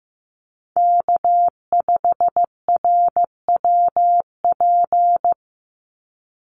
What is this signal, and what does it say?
Morse code, heard as a keyed tone, about 15 wpm, 700 Hz. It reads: K5RWP
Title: NA SOTA Chaser Callsigns 15wpm